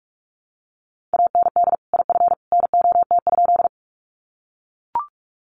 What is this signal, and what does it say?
Morse code, heard as a keyed tone, about 44 wpm, 700 Hz. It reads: AND IF NOT?
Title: Sentences from Top 600 Words 45wpm